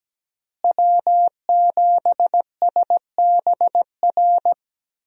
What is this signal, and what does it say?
Morse code, heard as a keyed tone, about 17 wpm, 700 Hz. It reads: W7SBR
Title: USA Call Signs - 5 Characters - Repeated 3x - 17wpm